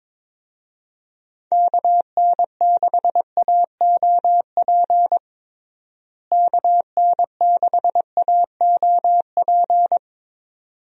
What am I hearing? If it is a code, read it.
Morse code, heard as a keyed tone, about 22 wpm, 700 Hz. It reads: KN6AOP KN6AOP
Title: USA Call Signs - 6 Characters - Repeated 2x - 22wpm